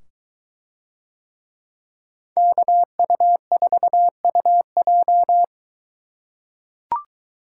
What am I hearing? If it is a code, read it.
Morse code, heard as a keyed tone, about 22 wpm, 700 Hz. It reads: KU4UJ